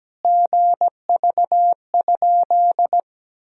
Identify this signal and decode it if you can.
Morse code, heard as a keyed tone, about 17 wpm, 700 Hz. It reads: GV?
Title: Sentences from Top 200 Words and Abbreviations 17wpm